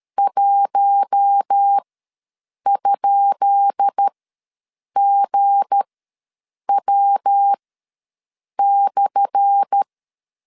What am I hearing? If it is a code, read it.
Morse code, heard as a keyed tone, about 13 wpm, spaced as for 7 wpm, 800 Hz. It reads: 1?GW/
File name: LEZIONI / 10 SIGNS / 10C DOT ws 12wpm 3548